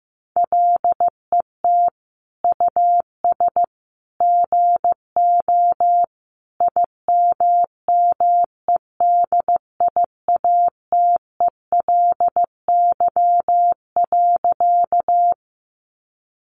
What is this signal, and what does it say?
Morse code, heard as a keyed tone, about 15 wpm, 700 Hz. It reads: LET US GO IMMEDIATELY.
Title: Sentences from Top 1100 Words 15wpm